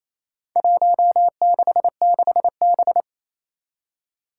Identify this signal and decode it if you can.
Morse code, heard as a keyed tone, about 28 wpm, 700 Hz. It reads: 166B
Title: Four Challenging Letter-Number Combinations 28wpm